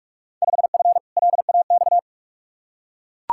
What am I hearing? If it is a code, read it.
Morse code, heard as a keyed tone, about 42 wpm, 700 Hz. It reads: 5F LAX
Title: Field Day Exchanges Encore - 45wpm